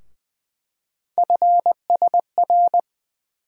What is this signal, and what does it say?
Morse code, heard as a keyed tone, about 20 wpm, 700 Hz. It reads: FSR
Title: USA Call Sign Postfixes - 3 Letters 20wpm